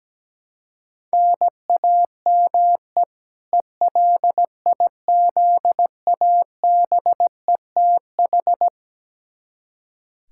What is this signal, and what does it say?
Morse code, heard as a keyed tone, about 17 wpm, 700 Hz. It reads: NAME ELIZABETH
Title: K1USN SST QSO Practice 17wpm